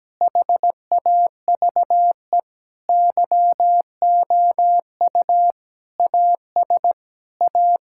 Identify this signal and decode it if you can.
Morse code, heard as a keyed tone, about 17 wpm, 700 Hz. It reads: HAVE YOU AS A
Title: Sentences from Top 1000 Words - Review Part 6 - 17wpm